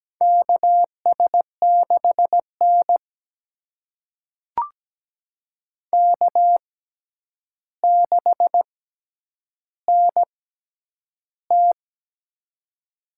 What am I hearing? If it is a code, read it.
Morse code, heard as a keyed tone, about 17 wpm, spaced as for 9 wpm, 700 Hz. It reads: KS6N K6NT
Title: USA Call Signs - 4 Characters - 6x Character Spacing - 17wpm